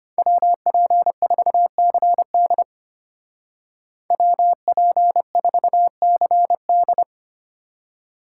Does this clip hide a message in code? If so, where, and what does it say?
Yes: Morse code from 0.2 to 7.0 s, heard as a keyed tone, about 32 wpm, 700 Hz. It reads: WP4CD WP4CD